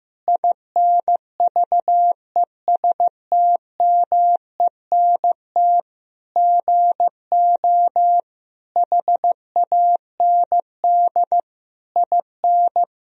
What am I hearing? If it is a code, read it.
Morse code, heard as a keyed tone, about 15 wpm, 700 Hz. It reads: INVESTMENT GO HAND IN
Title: Sentences from Top 1000 Words - Review Part 4 - 15wpm